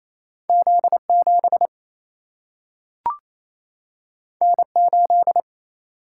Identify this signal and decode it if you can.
Morse code, heard as a keyed tone, about 28 wpm, 700 Hz. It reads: Z 7 N 8
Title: Single-Letter Single-Number 28wpm